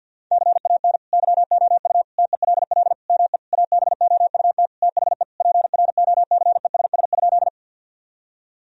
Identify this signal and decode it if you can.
Morse code, heard as a keyed tone, about 50 wpm, 700 Hz. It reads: CAN YOU TELL ME ABOUT THE PROCESS?
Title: Sentences from Top 1000 Words - Review Part 3 - 50wpm